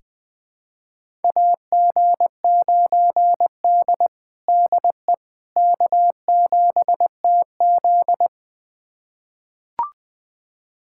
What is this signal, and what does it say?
Morse code, heard as a keyed tone, about 20 wpm, 700 Hz. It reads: AG9D DE K7TZ
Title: USA Callsign DE Callsign - 4 Characters - 20wpm